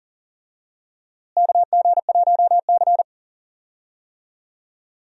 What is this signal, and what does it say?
Morse code, heard as a keyed tone, about 38 wpm, 700 Hz. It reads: KG1C
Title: USA Call Signs - 4 Characters - Repeated 3x - 40wpm